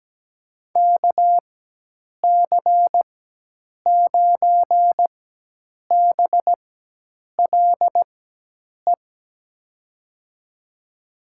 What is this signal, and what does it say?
Morse code, heard as a keyed tone, about 17 wpm, spaced as for 8 wpm, 700 Hz. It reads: KC9BLE
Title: USA Call Signs - 6 Characters - 4x Character Spacing - 17wpm